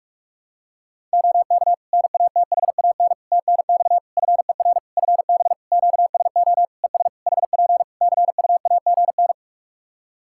Morse code, heard as a keyed tone, about 41 wpm, 700 Hz. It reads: OK NATHAN TNX FER FB QSO ES HP CUAGN
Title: Practice QSO - Exchange 4 - Ending - 45wpm